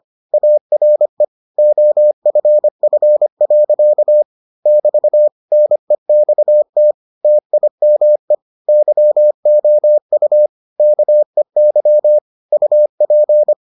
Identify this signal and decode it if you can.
Morse code, heard as a keyed tone, about 25 wpm, 600 Hz. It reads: ARE OFF. <BT> NEXT TIME YOU KEY UP